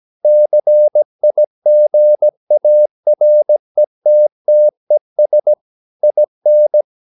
Morse code, heard as a keyed tone, about 17 wpm, 600 Hz. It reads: CIGARETTES IN
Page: Romanen Never af Ken Follet på engelsk.